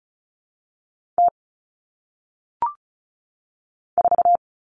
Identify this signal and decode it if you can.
Morse code, heard as a keyed tone, about 36 wpm, 700 Hz. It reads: T 4
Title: Common Cut Numbers - 1-Digit 35wpm